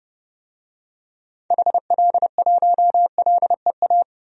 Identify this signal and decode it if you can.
Morse code, heard as a keyed tone, about 30 wpm, 700 Hz. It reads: HL1LEA
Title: DXCC-100 Call Signs - 6 Characters - 30wpm